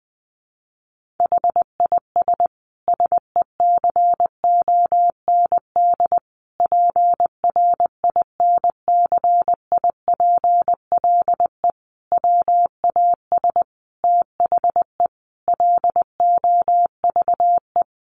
Morse code, heard as a keyed tone, about 20 wpm, 700 Hz. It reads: HIS SECOND PRINCIPLE WAS THE LOVE
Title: Sentences from Top 1400 Words 20wpm